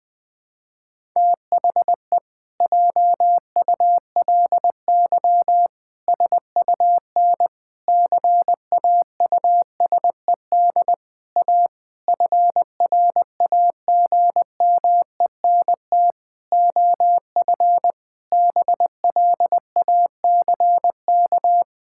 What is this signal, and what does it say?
Morse code, heard as a keyed tone, about 20 wpm, 700 Hz. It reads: THE JULY SUN CAUSED A FRAGMENT OF BLACK
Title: Pangram Challenge v2 - 20wpm